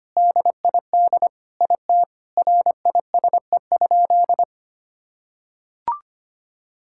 Morse code, heard as a keyed tone, about 25 wpm, 700 Hz. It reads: DID IT RISE?
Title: Sentences from Top 1000 Words - Review Part 6 - 25wpm